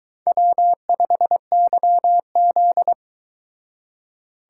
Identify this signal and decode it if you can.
Morse code, heard as a keyed tone, about 22 wpm, 700 Hz. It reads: W5YZ